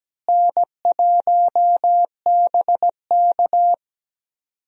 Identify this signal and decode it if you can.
Morse code, heard as a keyed tone, about 17 wpm, 700 Hz. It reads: N1BK
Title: USA Call Signs - 4 Characters - Repeated 2x - 17wpm